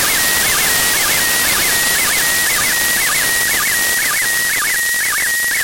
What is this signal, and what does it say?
SSTV (slow-scan television), PD 120.
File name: ISS-SSTV-audio z 04-12-14 12-11-2024